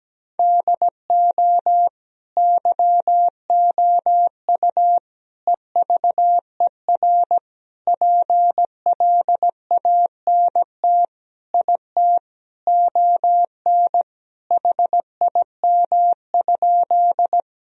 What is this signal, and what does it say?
Morse code, heard as a keyed tone, about 17 wpm, 700 Hz. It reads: DO YOU EVER PLANT IT ON HIM?
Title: Sentences from Top 1000 Words - Review Part 1 - 17wpm